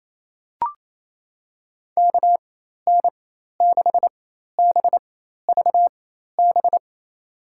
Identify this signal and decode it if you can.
Morse code, heard as a keyed tone, about 28 wpm, spaced as for 13 wpm, 700 Hz. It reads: KN6BVB
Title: USA Call Signs - 6 Characters - 4x Character Spacing - 28wpm